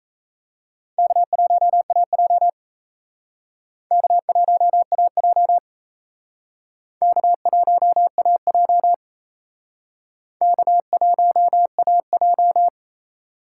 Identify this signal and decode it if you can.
Morse code, heard as a keyed tone, about 38 wpm, 700 Hz. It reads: K1AJ K1AJ K1AJ K1AJ